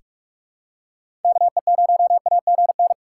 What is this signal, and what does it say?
Morse code, heard as a keyed tone, about 42 wpm, 690 Hz. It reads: KE0AGN
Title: USA Call Signs - 6 Characters - 3x Character Spacing - 45wpm